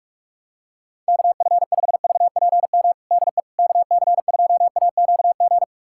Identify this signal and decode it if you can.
Morse code, heard as a keyed tone, about 46 wpm, 700 Hz. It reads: KF5VPM DE KC2AQG
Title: USA Callsign DE Callsign - 6 Characters - 45wpm